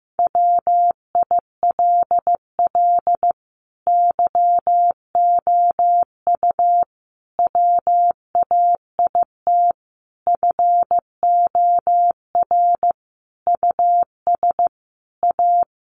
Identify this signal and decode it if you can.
Morse code, heard as a keyed tone, about 15 wpm, 700 Hz. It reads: WILL YOU WAIT FOR US A
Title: Sentences from Top 900 Words 15wpm